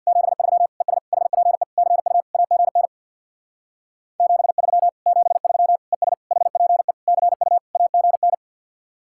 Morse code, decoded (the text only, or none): 73 ES HPE CU AGN 73 73 ES HPE CU AGN